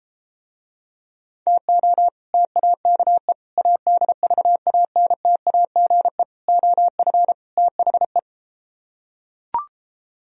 Morse code, heard as a keyed tone, about 33 wpm, 700 Hz. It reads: TO TAKE ADVANTAGE OF THE